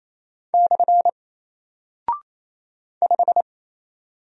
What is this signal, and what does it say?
Morse code, heard as a keyed tone, about 27 wpm, 700 Hz. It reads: / 5
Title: Single Character (Letter:Number:Punctuation:Prosign) 28wpm